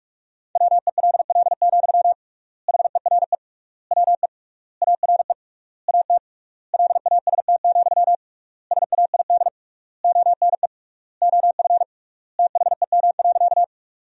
Morse code, heard as a keyed tone, about 46 wpm, 700 Hz. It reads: WELL, HERE WE ARE AT LAST, SAID ONE OF THEM.
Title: 3x Word-Spacing - Sentences from Top 300 Words - 45wpm